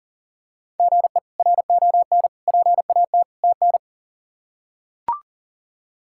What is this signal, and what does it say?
Morse code, heard as a keyed tone, about 40 wpm, 700 Hz. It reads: GE RON PAT TN
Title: K1USN SST QSO Practice - 40wpm